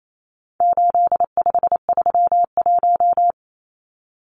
Morse code, heard as a keyed tone, about 28 wpm, 700 Hz. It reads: 8531
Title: Numbers - 4-Digit 28wpm